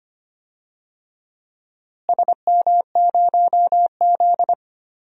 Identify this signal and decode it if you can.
Morse code, heard as a keyed tone, about 25 wpm, 700 Hz. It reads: SM0Z
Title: DXCC-100 Call Signs - 4 Characters - Repeated 3x - 25wpm